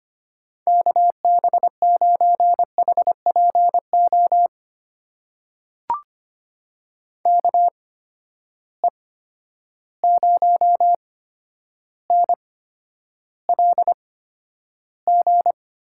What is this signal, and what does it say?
Morse code, heard as a keyed tone, about 25 wpm, spaced as for 11 wpm, 700 Hz. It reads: KB9HPO KE0NLG
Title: USA Call Signs - 6 Characters - 8x Character Spacing - 25wpm